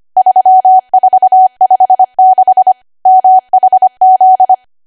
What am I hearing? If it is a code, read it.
Morse code, heard as a keyed tone, about 25 wpm, 750 Hz. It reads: 3456 MHZ
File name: cwa / W1AW-Level3 / 25WPM